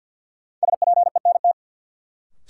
Morse code, heard as a keyed tone, about 53 wpm, 700 Hz. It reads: SPENT
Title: Sets of 5 Words - New Word Drill - 50wpm